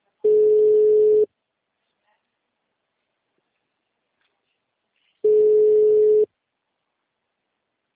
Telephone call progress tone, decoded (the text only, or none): ringback tone